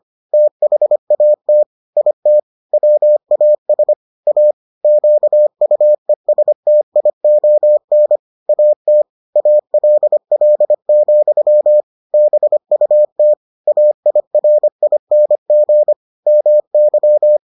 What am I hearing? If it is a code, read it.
Morse code, heard as a keyed tone, about 25 wpm, 600 Hz. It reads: THAT IT WAS A QUESTION AT ALL, BUT AIRING MY